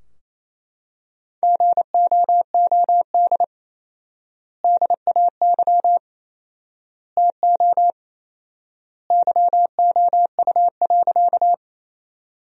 Morse code, heard as a keyed tone, about 28 wpm, 700 Hz. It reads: GOOD DAY TO YOU.
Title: 4x Word-Spacing - Sentences from Top 200 Words - 28wpm